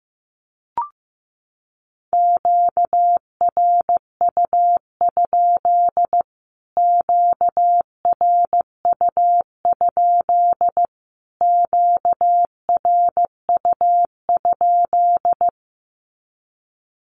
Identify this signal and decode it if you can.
Morse code, heard as a keyed tone, about 15 wpm, 700 Hz. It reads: QRU? QRU? QRU?
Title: Q-Codes Repeated 3x 15wpm